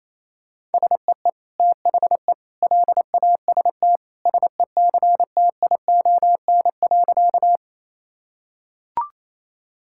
Morse code, heard as a keyed tone, about 28 wpm, 700 Hz. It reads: SEE THE LAST SECTION.
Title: Sentences from Top 1000 Words - Review Part 5 - 28wpm